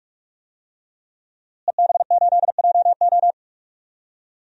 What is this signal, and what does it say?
Morse code, heard as a keyed tone, about 41 wpm, 700 Hz. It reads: EB8JO
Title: DXCC-100 Call Signs - 5 Characters - Repeated 2x - 45wpm